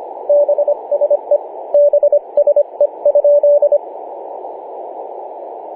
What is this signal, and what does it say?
Morse code, heard as a keyed tone, about 25 wpm, 580 Hz, with radio noise. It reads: BSE BSE?
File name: posts / oct-08-2033z-tx5z-21-008mhz-cw